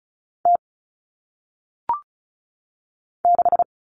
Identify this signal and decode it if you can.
Morse code, heard as a keyed tone, about 36 wpm, 700 Hz. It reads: T 6
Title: All Cut Numbers - 1-Digit 35wpm